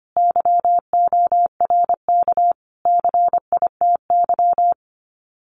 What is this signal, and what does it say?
Morse code, heard as a keyed tone, about 25 wpm, 700 Hz. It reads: YORK CITY
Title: Sets of 5 Words Encore2 25wpm